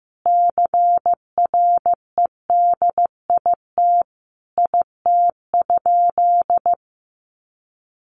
Morse code, heard as a keyed tone, about 15 wpm, 700 Hz. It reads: CREDIT IT?
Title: Sentences from Top 1000 Words 15wpm